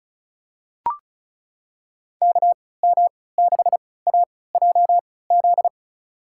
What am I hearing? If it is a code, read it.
Morse code, heard as a keyed tone, about 36 wpm, spaced as for 20 wpm, 700 Hz. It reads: KM6AJZ
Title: USA Call Signs - 6 Characters - 3x Character Spacing - 35wpm